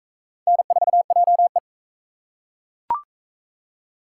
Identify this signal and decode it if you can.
Morse code, heard as a keyed tone, about 38 wpm, 700 Hz. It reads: N4JE